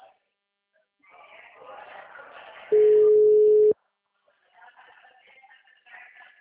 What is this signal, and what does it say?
Ringback tone (Europe, 425 Hz), with speech.